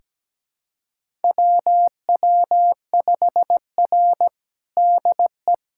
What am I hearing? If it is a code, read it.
Morse code, heard as a keyed tone, about 17 wpm, 700 Hz. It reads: WW5R DE
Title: USA Callsign DE Callsign - 4 Characters - 17wpm